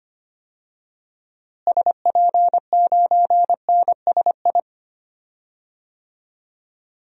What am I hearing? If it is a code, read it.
Morse code, heard as a keyed tone, about 25 wpm, 700 Hz. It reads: SP9NSI
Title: DXCC-100 Call Signs - 6 Characters - Repeated 2x - 25wpm